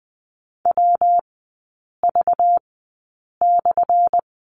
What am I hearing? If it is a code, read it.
Morse code, heard as a keyed tone, about 20 wpm, spaced as for 8 wpm, 700 Hz. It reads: WV/
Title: 191 Lesson 31e - Letter v - Three Characters 20wpm